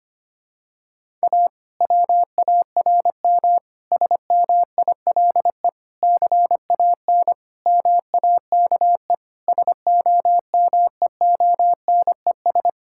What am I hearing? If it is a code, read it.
Morse code, heard as a keyed tone, about 25 wpm, 700 Hz. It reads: A WARM SMILE CAN MAKE SOMEONES
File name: practice / 25wpm / Difficult Sentences with Hints - 25wpm